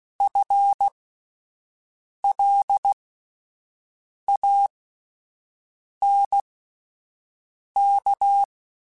Morse code, heard as a keyed tone, about 16 wpm, spaced as for 5 wpm, 800 Hz. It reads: FLANK
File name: Spaced / FL Runs sp mp3s / FL Runs sp @10